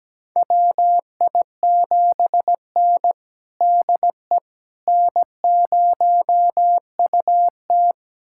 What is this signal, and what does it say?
Morse code, heard as a keyed tone, about 17 wpm, 700 Hz. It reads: WI7N DE N0UT
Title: USA Callsign DE Callsign - 4 Characters - 17wpm